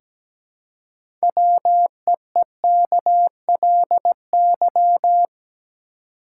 Morse code, heard as a keyed tone, about 17 wpm, 700 Hz. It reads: WEEKLY
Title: Six Letter Words - 17wpm